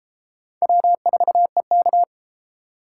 Morse code, heard as a keyed tone, about 29 wpm, 700 Hz. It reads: W4EK